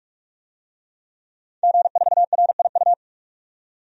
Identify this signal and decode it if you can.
Morse code, heard as a keyed tone, about 46 wpm, 700 Hz. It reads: G4RIU
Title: DXCC-100 Call Signs - 5 Characters - 45wpm